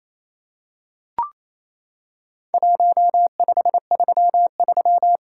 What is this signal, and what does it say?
Morse code, heard as a keyed tone, about 28 wpm, 700 Hz. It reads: 1533
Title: Worked All Europe DX Contest - QTC Exchange - 28wpm